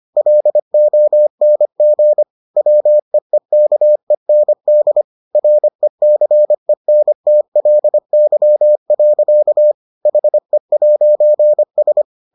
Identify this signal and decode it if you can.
Morse code, heard as a keyed tone, about 25 wpm, 600 Hz. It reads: LONG WEEKEND RECENTLY. HE'S